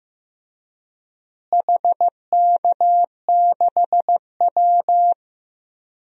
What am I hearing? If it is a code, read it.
Morse code, heard as a keyed tone, about 15 wpm, 700 Hz. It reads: HK6W
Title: DXCC-100 Call Signs - 4 Characters - Repeated 2x - 15wpm